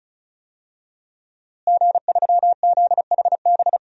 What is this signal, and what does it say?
Morse code, heard as a keyed tone, about 34 wpm, 700 Hz. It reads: G3ZHB
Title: DXCC-100 Call Signs - 5 Characters - Repeated 3x - 35wpm